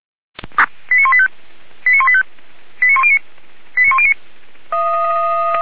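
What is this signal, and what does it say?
Five-tone selective call (ZVEI), heard as quick runs of five tones, tones of about 70 ms. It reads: 89185 89185 89109 89109
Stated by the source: ZVEI five-tone selective call received by SDR (464).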